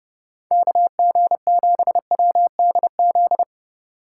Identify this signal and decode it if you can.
Morse code, heard as a keyed tone, about 31 wpm, 700 Hz. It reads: KG7WDZ
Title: USA Call Signs - 6 Characters - 8x Character Spacing - 30wpm